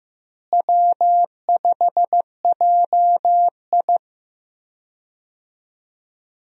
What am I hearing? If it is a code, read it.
Morse code, heard as a keyed tone, about 15 wpm, 700 Hz. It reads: W5JI
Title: USA Call Signs - 4 Characters - Repeated 3x - 15wpm